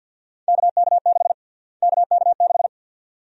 Morse code, heard as a keyed tone, about 52 wpm, 700 Hz. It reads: KK6 KK6
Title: USA Call Sign Prefixes - Repeated 2x - 50wpm